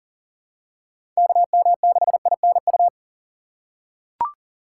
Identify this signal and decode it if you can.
Morse code, heard as a keyed tone, about 39 wpm, 700 Hz. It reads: KM6INU
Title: USA Call Signs - 6 Characters - 4x Character Spacing - 40wpm